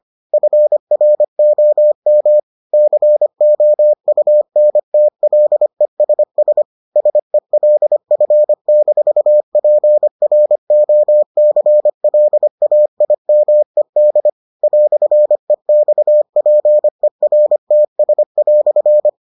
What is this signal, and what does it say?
Morse code, heard as a keyed tone, about 25 wpm, 600 Hz. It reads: FROM COUNTLESS SELF-PROCLAIMED "EXPERTS"